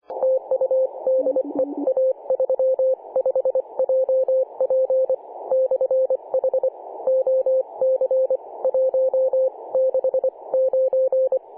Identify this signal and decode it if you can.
Morse code, heard as a keyed tone, about 24 wpm, spaced as for 21 wpm, 520 Hz. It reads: TU DE A35JP/H OC 169
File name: 2013/Sep/22 2149z A35JP/H 21.010MHz CW